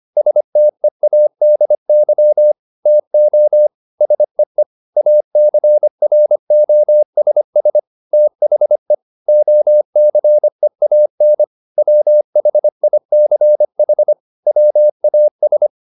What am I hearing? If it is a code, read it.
Morse code, heard as a keyed tone, about 25 wpm, 600 Hz. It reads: STEADY TO SEE ACROSS THE OCEAN WHICH WAS